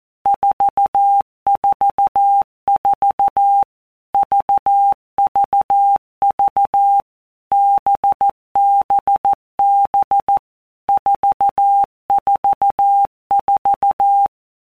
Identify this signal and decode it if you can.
Morse code, heard as a keyed tone, about 14 wpm, 800 Hz. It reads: 444 VVV BBB 444
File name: Basic Letter Groups By Blocks / Groups@13 / VB46@13